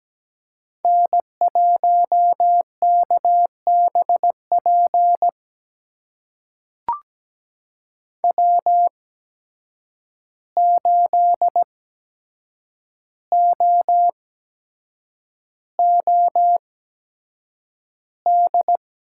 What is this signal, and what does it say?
Morse code, heard as a keyed tone, about 17 wpm, spaced as for 7 wpm, 700 Hz. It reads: N1KBP W8OOD
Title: USA Call Signs - 5 Characters - 8x Character Spacing - 17wpm